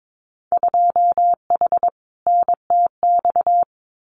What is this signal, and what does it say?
Morse code, heard as a keyed tone, about 22 wpm, 700 Hz. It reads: 2H NTX
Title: Winter Field Day Exchanges Encore 22wpm